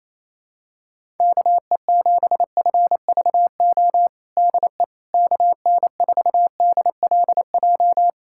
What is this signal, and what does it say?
Morse code, heard as a keyed tone, about 28 wpm, 700 Hz. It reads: KE7FVO DE KN4DLJ
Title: USA Callsign DE Callsign - 6 Characters - 28wpm